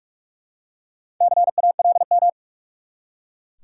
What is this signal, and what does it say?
Morse code, heard as a keyed tone, about 43 wpm, 700 Hz. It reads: CALM
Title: Four Letter Words - 45wpm